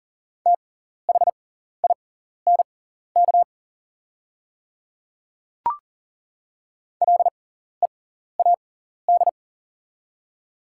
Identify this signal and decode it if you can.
Morse code, heard as a keyed tone, about 40 wpm, spaced as for 14 wpm, 700 Hz. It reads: THINK LEAD
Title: 6x Character Spacing - Top 500 Words - 40wpm